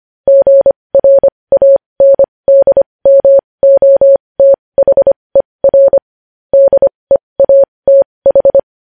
Morse code, heard as a keyed tone, about 25 wpm, spaced as for 20 wpm, 550 Hz. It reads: GRANDMOTHER DEATH
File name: podcast / News headlines-1x-25x20